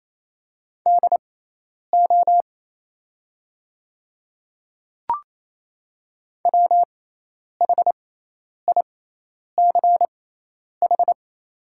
Morse code, heard as a keyed tone, about 28 wpm, spaced as for 10 wpm, 700 Hz. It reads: DO WHICH
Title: 6x Character Spacing - Top 100 Words - 28wpm